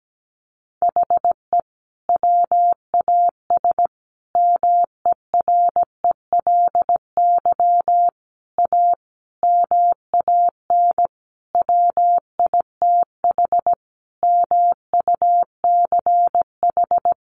Morse code, heard as a keyed tone, about 17 wpm, 700 Hz. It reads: HE WAS MERELY A MAN WITH MUCH
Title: Sentences from Top 1400 Words 17wpm